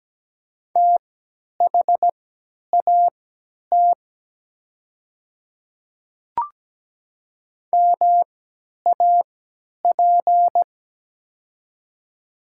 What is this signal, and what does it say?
Morse code, heard as a keyed tone, about 17 wpm, spaced as for 10 wpm, 700 Hz. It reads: THAT MAP
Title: 3x Character Spacing - Top 500 Words - 17wpm